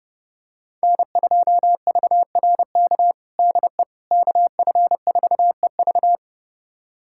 Morse code, heard as a keyed tone, about 30 wpm, 700 Hz. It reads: N2VRK DE KF4EV
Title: USA Callsign DE Callsign - 5 Characters - 30wpm